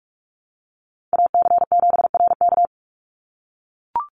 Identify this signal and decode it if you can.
Morse code, heard as a keyed tone, about 44 wpm, 700 Hz. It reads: AC7RK
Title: USA Call Signs 45wpm